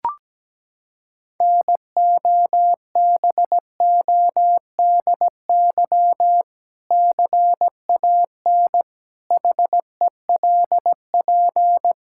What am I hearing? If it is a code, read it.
Morse code, heard as a keyed tone, about 17 wpm, 700 Hz. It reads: NOBODY CAN HELP
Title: Sentences from Top 1600 Words 17wpm